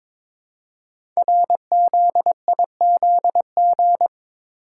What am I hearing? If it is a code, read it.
Morse code, heard as a keyed tone, about 22 wpm, 700 Hz. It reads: RZIZG
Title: Code Groups - 5 Letters - 22wpm